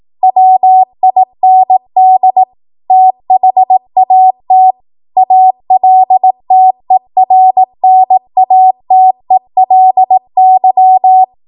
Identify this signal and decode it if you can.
Morse code, heard as a keyed tone, about 18 wpm, 750 Hz. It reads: WIND THAT ALTERNATELY
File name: Morse / 18WPM